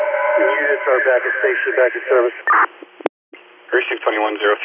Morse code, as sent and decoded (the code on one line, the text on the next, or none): none